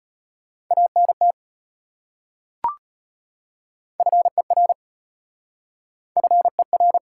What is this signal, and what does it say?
Morse code, heard as a keyed tone, about 33 wpm, 700 Hz. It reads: ANT FER FER